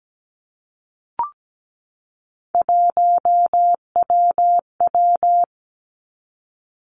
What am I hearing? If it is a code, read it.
Morse code, heard as a keyed tone, about 17 wpm, 700 Hz. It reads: 1WW
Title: Three Challenging Letter-Number Combinations 17wpm